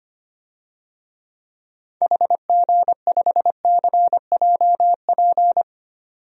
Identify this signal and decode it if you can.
Morse code, heard as a keyed tone, about 25 wpm, 700 Hz. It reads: HG5CJP
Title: DXCC-100 Call Signs - 6 Characters - 25wpm